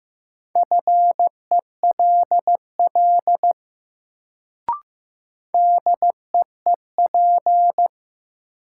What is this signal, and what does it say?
Morse code, heard as a keyed tone, about 15 wpm, 700 Hz. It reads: FELL DEEP
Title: Sets of 5 Words Encore - New Word Drill - 15wpm